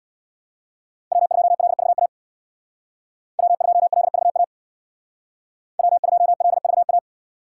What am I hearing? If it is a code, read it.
Morse code, heard as a keyed tone, about 64 wpm, 700 Hz. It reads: W2LVA W2LVA W2LVA